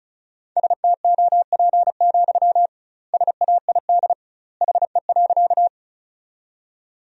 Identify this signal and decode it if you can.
Morse code, heard as a keyed tone, about 34 wpm, 700 Hz. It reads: STOP, SAID HE.
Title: 2x Word-Spacing - Sentences from Top 300 Words - 35wpm